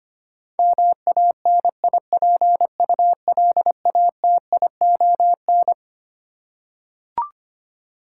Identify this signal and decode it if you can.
Morse code, heard as a keyed tone, about 25 wpm, 700 Hz. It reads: MANIPULATION
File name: Twelve Letter Words - 25wpm